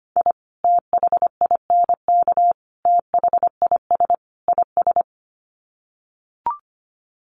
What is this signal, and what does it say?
Morse code, heard as a keyed tone, about 25 wpm, 700 Hz. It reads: I THINK THIS IS
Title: Sets of 4 Words 25wpm